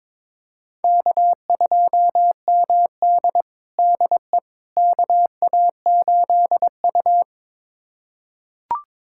Morse code, heard as a keyed tone, about 22 wpm, 700 Hz. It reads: K2MD DE KA8U
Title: USA Callsign DE Callsign - 4 Characters - 22wpm